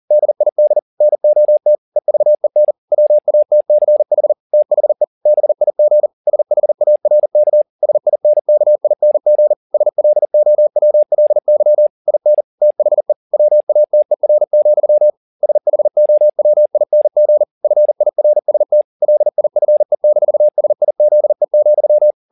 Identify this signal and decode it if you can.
Morse code, heard as a keyed tone, about 40 wpm, 600 Hz. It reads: DID NOT EVEN WATCH THE BIG SHARK SINKING SLOWLY IN THE WATER, SHOWING FIRST LIFE-SIZE,
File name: the old man and the sea 2 40wpm